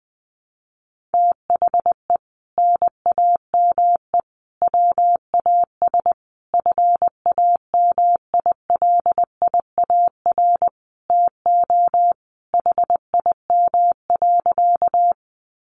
Morse code, heard as a keyed tone, about 20 wpm, 700 Hz. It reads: THE NAME WAS FAMILIAR TO HIM.
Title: Sentences from Top 1000 Words 20wpm